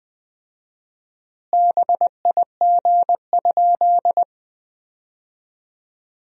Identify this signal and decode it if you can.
Morse code, heard as a keyed tone, about 20 wpm, 700 Hz. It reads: BIG?
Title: 4x Word-Spacing - Sentences from Top 200 Words - 20wpm